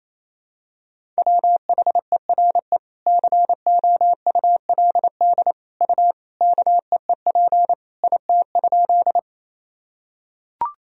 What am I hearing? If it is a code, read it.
Morse code, heard as a keyed tone, about 28 wpm, 700 Hz. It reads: WHERE COULD U KEEP IT?
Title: Sentences from Top 300 Words and Abbreviations 28wpm